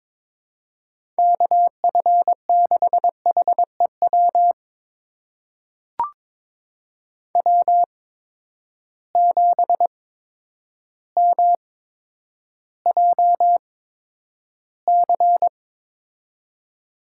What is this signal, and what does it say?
Morse code, heard as a keyed tone, about 22 wpm, spaced as for 10 wpm, 700 Hz. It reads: KF6HEW W7MJC
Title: USA Call Signs - 8x Character Spacing - 22wpm